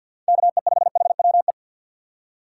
Morse code, heard as a keyed tone, about 53 wpm, 700 Hz. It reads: KE5HWE
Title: USA Call Signs - 6 Characters - 8x Character Spacing - 50wpm